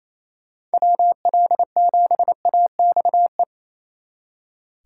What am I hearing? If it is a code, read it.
Morse code, heard as a keyed tone, about 33 wpm, 700 Hz. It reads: WL7AXE
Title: Speed Racing - USA 2x3 Call Signs 28wpm